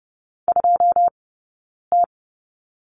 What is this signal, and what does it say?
Morse code, heard as a keyed tone, about 30 wpm, spaced as for 9 wpm, 690 Hz. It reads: 2T
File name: Lesson 17d - Letter u - Two Characters 30wpm